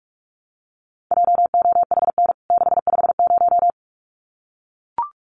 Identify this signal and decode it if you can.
Morse code, heard as a keyed tone, about 44 wpm, 700 Hz. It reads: JOHN 650
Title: CWT Exchange 45wpm